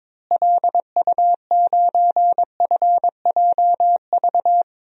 Morse code, heard as a keyed tone, about 21 wpm, 700 Hz. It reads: LU9FJV
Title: DXCC-100 Call Signs - 6 Characters - Repeated 3x - 22wpm